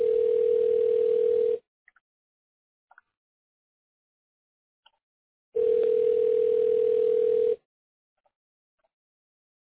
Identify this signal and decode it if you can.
Ringback tone (North America), 440 + 480 Hz.